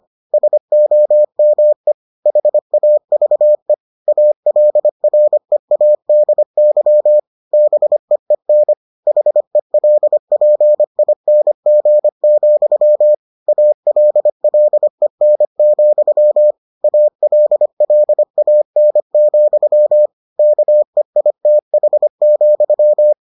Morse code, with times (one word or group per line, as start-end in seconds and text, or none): 0.3-1.9 SOME
2.3-3.7 HAVE
4.1-7.2 ALREADY
7.5-8.7 BEEN
9.1-13.1 HELPING,
13.5-16.5 ALLEN,
16.8-20.1 ALLAN,
20.4-23.1 KEITH,